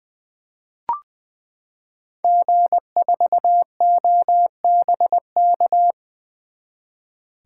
Morse code, heard as a keyed tone, about 20 wpm, 700 Hz. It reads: G4OBK
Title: 2023 SOTA Chaser Call Signs - 20wpm